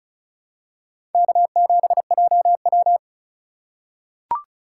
Morse code, heard as a keyed tone, about 36 wpm, 700 Hz. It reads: K7JW
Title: USA Call Signs - 4 Characters - 4x Character Spacing - 35wpm